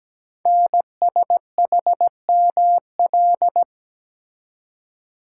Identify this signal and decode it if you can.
Morse code, heard as a keyed tone, about 17 wpm, 700 Hz. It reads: NSHML
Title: Code Groups - 5 Letters - 17wpm